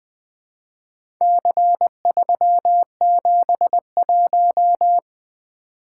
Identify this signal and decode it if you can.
Morse code, heard as a keyed tone, about 24 wpm, 700 Hz. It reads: C371
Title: Speed-Racing - Four Letter-Number Combinations - 20wpm